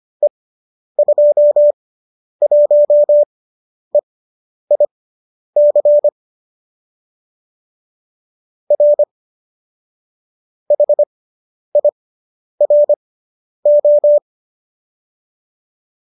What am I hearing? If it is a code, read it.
Morse code, heard as a keyed tone, about 25 wpm, spaced as for 10 wpm, 570 Hz. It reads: E21EIC R HIRO